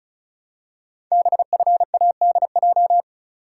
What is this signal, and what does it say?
Morse code, heard as a keyed tone, about 36 wpm, 700 Hz. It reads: BFADJ
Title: Code Groups - 5 Letters - 35wpm